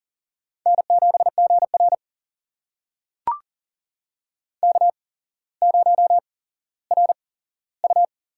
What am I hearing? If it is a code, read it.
Morse code, heard as a keyed tone, about 40 wpm, spaced as for 17 wpm, 700 Hz. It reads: N7GR K0RU
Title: USA Call Signs - 4 Characters - 8x Character Spacing - 40wpm